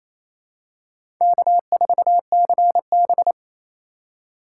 Morse code, heard as a keyed tone, about 28 wpm, 700 Hz. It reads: K4CB
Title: USA Call Signs - 4 Characters - 4x Character Spacing - 28wpm